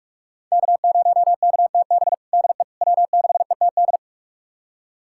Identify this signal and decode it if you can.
Morse code, heard as a keyed tone, about 46 wpm, 700 Hz. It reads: K0KTB DE W6ETD
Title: USA Callsign DE Callsign - 5 Characters - 45wpm